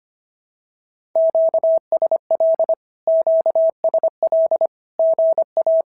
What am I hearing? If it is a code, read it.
Morse code, heard as a keyed tone, about 25 wpm, 650 Hz. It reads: QSL QSL GA
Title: SOTA QSO Practice 25wpm